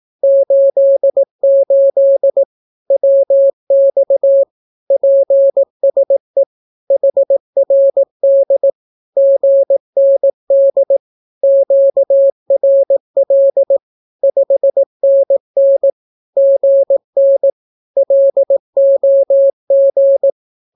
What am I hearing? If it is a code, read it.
Morse code, heard as a keyed tone, about 18 wpm, 550 Hz. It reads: 88 WX PSE HRD GND QRL 5NN GN LOG
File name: naukapodstaw / skroty / skroty